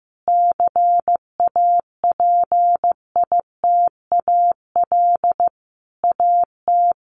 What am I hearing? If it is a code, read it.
Morse code, heard as a keyed tone, about 15 wpm, 700 Hz. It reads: CAPITAL AT
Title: Sentences from Top 1000 Words 15wpm